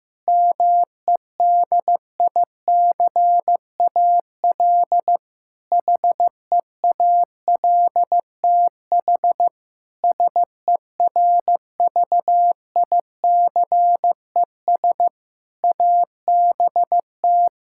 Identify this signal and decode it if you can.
Morse code, heard as a keyed tone, about 15 wpm, 700 Hz. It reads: MEDICAL HEALTH SERVICES ABT
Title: Practice QSO - Exchange 3 - More Info - 15wpm